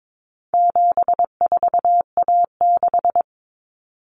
Morse code, heard as a keyed tone, about 22 wpm, 700 Hz. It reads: 74A6
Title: Common Cut Numbers - 4-Digit 22wpm